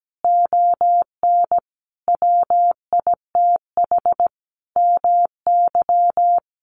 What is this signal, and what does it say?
Morse code, heard as a keyed tone, about 17 wpm, 700 Hz. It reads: ON WITH MY
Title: Sentences from Top 300 Words 17wpm